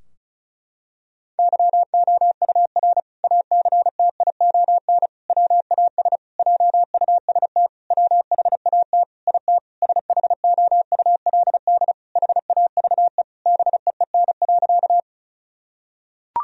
Morse code, heard as a keyed tone, about 36 wpm, 700 Hz. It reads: YOUR ACTION WAS JUST WHAT IT SHOULD HAVE BEEN.
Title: Sentences from Top 1000 Words - Review Part 4 - 35wpm